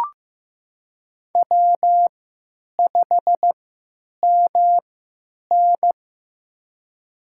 Morse code, heard as a keyed tone, about 15 wpm, spaced as for 9 wpm, 700 Hz. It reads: W5MN
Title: USA Call Signs - 4 Characters - 3x Character Spacing - 15wpm